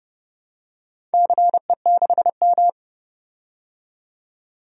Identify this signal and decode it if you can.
Morse code, heard as a keyed tone, about 30 wpm, 700 Hz. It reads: CE6M
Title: DXCC-100 Call Signs - 4 Characters - 30wpm